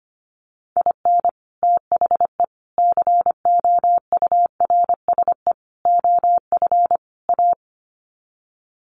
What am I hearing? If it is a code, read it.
Morse code, heard as a keyed tone, about 25 wpm, 700 Hz. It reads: IN THE COURSE OF A
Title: Sets of 5 Words Encore2 25wpm